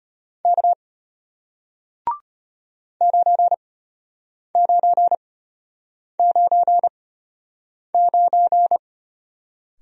Morse code, heard as a keyed tone, about 34 wpm, spaced as for 8 wpm, 700 Hz. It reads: K 9999